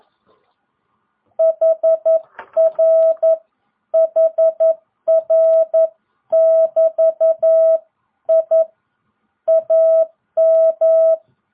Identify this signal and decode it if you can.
Morse code, heard as a keyed tone, about 11 wpm, 660 Hz. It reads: HR HR <BT> I AM